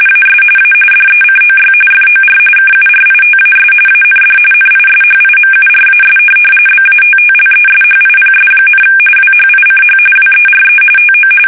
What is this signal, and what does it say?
RTTY, about 45.45 baud. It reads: AS A DECOY FOR KURITAS AND NISHIMURAS BATTLESHIPS, WHICH WERE TO
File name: fsk850 / USS-Massachusetts